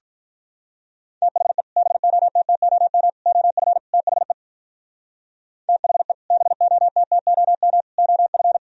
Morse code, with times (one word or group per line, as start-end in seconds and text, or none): 1.2-1.6 THE
1.8-3.1 BOTTOM
3.3-3.8 OF
3.9-4.3 THE
5.7-6.1 THE
6.3-7.8 BOTTOM
8.0-8.6 OF